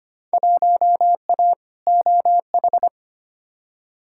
Morse code, heard as a keyed tone, about 25 wpm, 700 Hz. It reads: 1A OH
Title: Field Day Exchanges - 25wpm